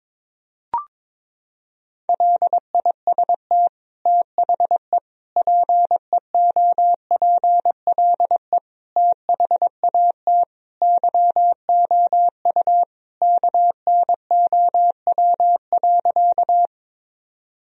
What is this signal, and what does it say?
Morse code, heard as a keyed tone, about 22 wpm, 700 Hz. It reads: LIST THE PEOPLE THAT YOU KNOW.
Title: Sentences from Top 1000 Words - Review Part 2 - 22wpm